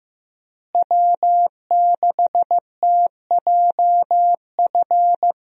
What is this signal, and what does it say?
Morse code, heard as a keyed tone, about 15 wpm, 700 Hz. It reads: W6TJF
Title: USA Call Signs - 5 Characters - Repeated 3x - 15wpm